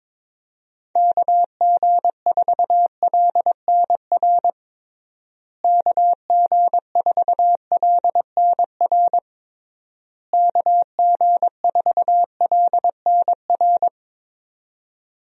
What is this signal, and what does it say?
Morse code, heard as a keyed tone, about 22 wpm, 700 Hz. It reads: KG4LNR KG4LNR KG4LNR
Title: USA Call Signs - 6 Characters - Repeated 3x - 22wpm